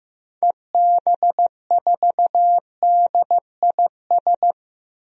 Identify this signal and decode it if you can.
Morse code, heard as a keyed tone, about 16 wpm, 700 Hz. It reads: EB4DIS
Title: DXCC-100 Call Signs - 6 Characters - Repeated 3x - 15wpm